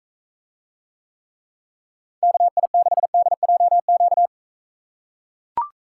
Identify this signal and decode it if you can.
Morse code, heard as a keyed tone, about 42 wpm, 700 Hz. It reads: KI6DJQ